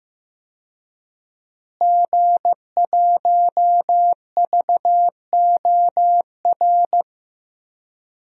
Morse code, heard as a keyed tone, about 15 wpm, 700 Hz. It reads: G1VOR
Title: DXCC-100 Call Signs - 5 Characters - Repeated 2x - 15wpm